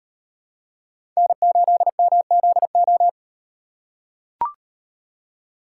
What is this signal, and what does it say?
Morse code, heard as a keyed tone, about 38 wpm, 700 Hz. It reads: N8MZO